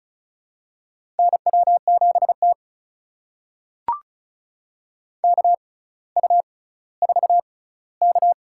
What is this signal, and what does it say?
Morse code, heard as a keyed tone, about 35 wpm, spaced as for 18 wpm, 700 Hz. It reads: NW7T KU4K
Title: USA Call Signs - 4 Characters - 6x Character Spacing - 35wpm